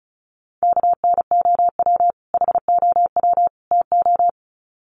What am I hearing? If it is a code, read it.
Morse code, heard as a keyed tone, about 34 wpm, 700 Hz. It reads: KNOW HOW TO
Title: Sets of 3 Words 35wpm